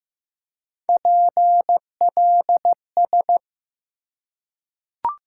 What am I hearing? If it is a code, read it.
Morse code, heard as a keyed tone, about 15 wpm, 700 Hz. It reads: PLS
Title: Top 90 QSO Elements 15wpm